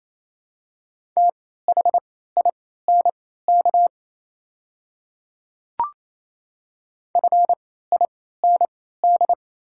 Morse code, heard as a keyed tone, about 28 wpm, spaced as for 16 wpm, 700 Hz. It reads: THINK FIND
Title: 3x Character Spacing - Top 500 Words - 28wpm